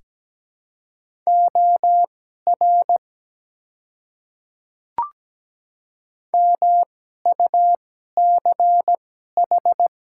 Morse code, heard as a keyed tone, about 17 wpm, spaced as for 12 wpm, 700 Hz. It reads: OR MUCH
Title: 2x Character Spacing - Top 300 Words - 17wpm